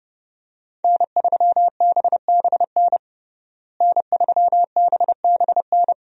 Morse code, heard as a keyed tone, about 30 wpm, 700 Hz. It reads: N3BBN N3BBN
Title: USA Call Signs - 5 Characters - Repeated 2x - 30wpm